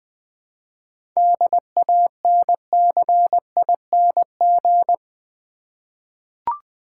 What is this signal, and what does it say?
Morse code, heard as a keyed tone, about 20 wpm, 700 Hz. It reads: DANCING
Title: 7 Letter Words - First 3 Letters Given - 20wpm